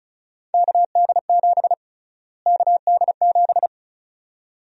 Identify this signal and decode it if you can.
Morse code, heard as a keyed tone, about 36 wpm, 700 Hz. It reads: KD7 KD7
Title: USA Call Sign Prefixes - Repeated 2x - 35wpm